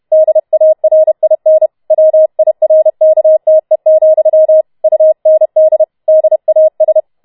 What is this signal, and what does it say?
Morse code, heard as a keyed tone, about 31 wpm, 630 Hz. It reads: DARIN WIRKTE, UND DAS
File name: Schneewittchen Teil4 32wpm